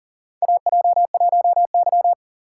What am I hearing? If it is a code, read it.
Morse code, heard as a keyed tone, about 42 wpm, 700 Hz. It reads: AJ1Y
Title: USA Call Signs - 4 Characters - 4x Character Spacing - 40wpm